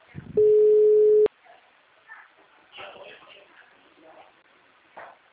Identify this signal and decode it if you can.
Ringback tone (Europe, 425 Hz), with speech.
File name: out- -2026